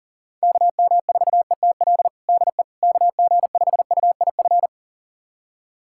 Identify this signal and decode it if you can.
Morse code, heard as a keyed tone, about 40 wpm, 700 Hz. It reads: KM4ETL DE KG5UIF
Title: USA Callsign DE Callsign - 6 Characters - 40wpm